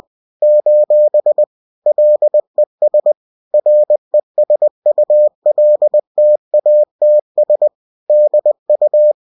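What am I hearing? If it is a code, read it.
Morse code, heard as a keyed tone, about 20 wpm, 600 Hz. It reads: : LES RESULTATS DU
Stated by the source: Bulletin sent in CW.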